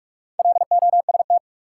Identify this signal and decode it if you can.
Morse code, heard as a keyed tone, about 43 wpm, 700 Hz. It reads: LOST
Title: Sentences from Top 1400 Words - New Word Drill Alt - 45wpm